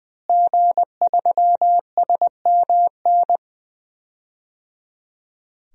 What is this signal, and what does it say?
Morse code, heard as a keyed tone, about 21 wpm, 700 Hz. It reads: G3SMN
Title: DXCC-100 Call Signs - 5 Characters - Repeated 3x - 20wpm